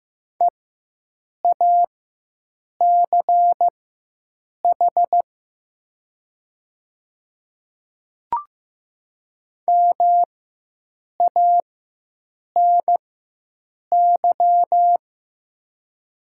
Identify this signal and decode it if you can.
Morse code, heard as a keyed tone, about 15 wpm, spaced as for 7 wpm, 700 Hz. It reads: EACH MANY
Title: 4x Character Spacing - Top 200 Words - 15wpm